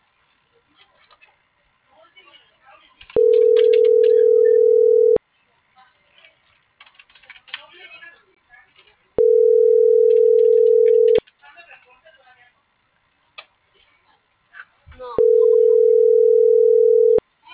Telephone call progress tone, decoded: ringback tone

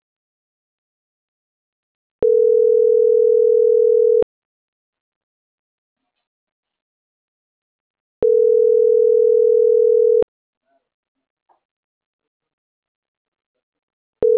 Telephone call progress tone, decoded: ringback tone

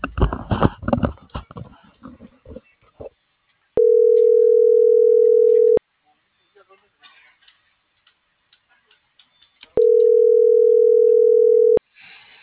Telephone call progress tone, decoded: ringback tone

